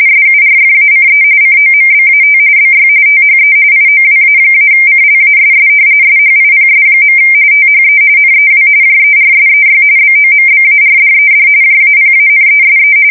RTTY, about 45.45 baud. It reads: SURFACE COMBAT BY THE IOWA-CLASS SHIPS. THE NEW JERSEY BOMBARDED JAPANESE
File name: fsk170 / battleshipnj